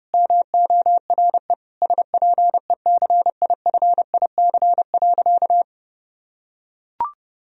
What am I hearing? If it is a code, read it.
Morse code, heard as a keyed tone, about 32 wpm, 700 Hz. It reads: MORE SPECIFIC.
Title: Sentences from Top 1000 Words - Review Part 3 - 30wpm